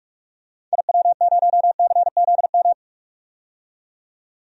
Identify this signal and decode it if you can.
Morse code, heard as a keyed tone, about 41 wpm, 700 Hz. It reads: IW0CZM
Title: DXCC-100 Call Signs - 6 Characters - 45wpm